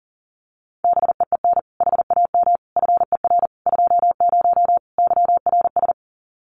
Morse code, heard as a keyed tone, about 40 wpm, 700 Hz. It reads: BEEN HAM FER 20 YRS
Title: Common QSO Phrases 40wpm